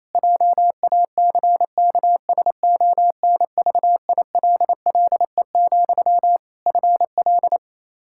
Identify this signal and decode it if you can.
Morse code, heard as a keyed tone, about 28 wpm, 700 Hz. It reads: JACKSONVILLE, FL
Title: Practice QSO - Exchange 1 - Beginning - 28wpm